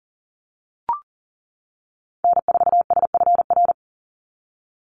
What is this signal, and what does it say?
Morse code, heard as a keyed tone, about 38 wpm, 700 Hz. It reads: N4SFR
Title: NA SOTA Activator Callsigns 40wpm